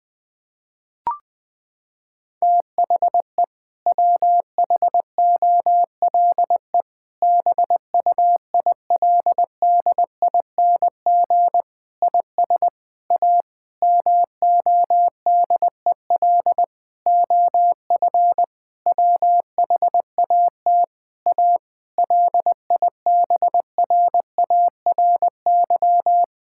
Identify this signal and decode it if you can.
Morse code, heard as a keyed tone, about 20 wpm, 700 Hz. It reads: THE WHOLE BUILDING IS A MODEL OF WHAT A LIBRARY
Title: Sentences from Top 1000 Words - Review Part 5 - 20wpm